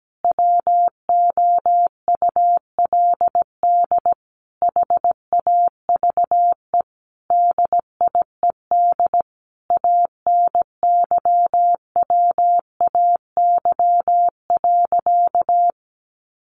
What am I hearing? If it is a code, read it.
Morse code, heard as a keyed tone, about 17 wpm, 700 Hz. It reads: WOULD HAVE DIED ANYWAY.
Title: A Sound Of Thunder 17wpm